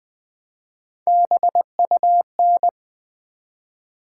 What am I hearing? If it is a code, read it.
Morse code, heard as a keyed tone, about 20 wpm, 700 Hz. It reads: BUN
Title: Build A Word - 1 Letter At A Time - 5-Letter Words - 20wpm